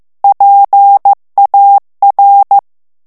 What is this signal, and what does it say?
Morse code, heard as a keyed tone, about 15 wpm, 800 Hz. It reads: PAR
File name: FRANCAIS TEXTE CLAIR 2 VITESSE 14 MOTS MINUTES